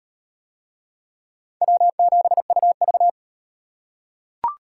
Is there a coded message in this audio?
Yes: Morse code from 1.6 to 3.1 s, heard as a keyed tone, about 38 wpm, 700 Hz. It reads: W7UV